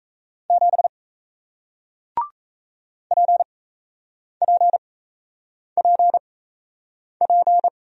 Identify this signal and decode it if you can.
Morse code, heard as a keyed tone, about 38 wpm, spaced as for 8 wpm, 700 Hz. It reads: 7 PPPP